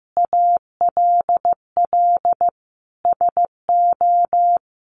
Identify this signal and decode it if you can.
Morse code, heard as a keyed tone, about 15 wpm, 700 Hz. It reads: ALL SO
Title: Sentences from Top 1300 Words 15wpm